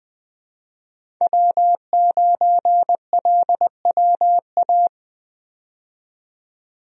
Morse code, heard as a keyed tone, about 20 wpm, 700 Hz. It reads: W9LWA
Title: USA Call Signs - 5 Characters - 20wpm